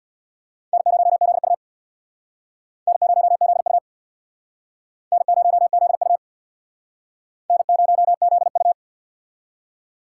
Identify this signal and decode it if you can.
Morse code, heard as a keyed tone, about 62 wpm, 700 Hz. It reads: N0ZU N0ZU N0ZU N0ZU
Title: Speed Racing - USA Call Signs 50wpm